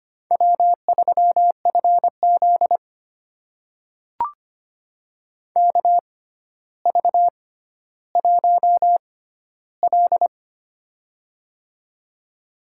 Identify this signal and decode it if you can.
Morse code, heard as a keyed tone, about 25 wpm, spaced as for 13 wpm, 700 Hz. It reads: W3FZ KV1L
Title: USA Call Signs - 4 Characters - 6x Character Spacing - 25wpm